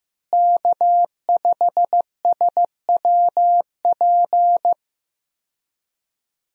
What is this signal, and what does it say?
Morse code, heard as a keyed tone, about 15 wpm, 700 Hz. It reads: K5SWP
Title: USA Call Signs - 5 Characters - Repeated 2x - 15wpm